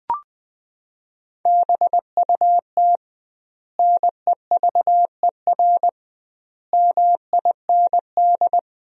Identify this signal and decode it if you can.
Morse code, heard as a keyed tone, about 20 wpm, 700 Hz. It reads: BUT NEVER MIND
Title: 2x Word-Spacing - Sentences from Top 500 Words - 20wpm